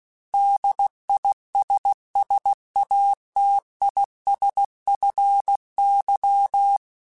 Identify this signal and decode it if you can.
Morse code, heard as a keyed tone, about 16 wpm, 800 Hz. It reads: DISSATISFY
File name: General / General-mp3s / ISS @15